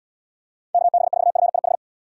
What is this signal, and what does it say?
Morse code, heard as a keyed tone, about 74 wpm, 700 Hz. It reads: KB4FIH